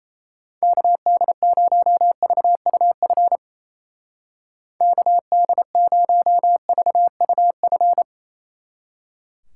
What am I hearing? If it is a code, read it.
Morse code, heard as a keyed tone, about 34 wpm, 700 Hz. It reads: KD0VUF KD0VUF
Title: Speed Racing - USA 2x3 Call Signs 28wpm